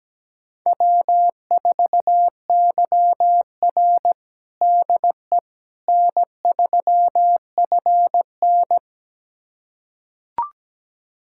Morse code, heard as a keyed tone, about 17 wpm, 700 Hz. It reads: W4YR DE N3FN
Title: USA Callsign DE Callsign - 4 Characters - 17wpm